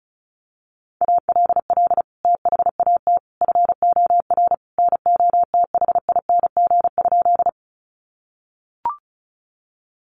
Morse code, heard as a keyed tone, about 34 wpm, 700 Hz. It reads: ALL THAT FOR NOTHING?
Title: Sentences from Top 500 Words 35wpm